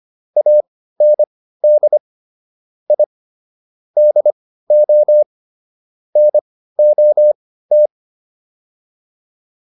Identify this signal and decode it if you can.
Morse code, heard as a keyed tone, about 25 wpm, spaced as for 15 wpm, 600 Hz. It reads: AND I DO NOT